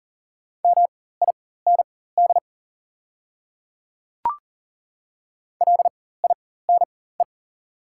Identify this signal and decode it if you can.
Morse code, heard as a keyed tone, about 40 wpm, spaced as for 19 wpm, 700 Hz. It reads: MIND LINE
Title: 4x Character Spacing - Top 500 Words - 40wpm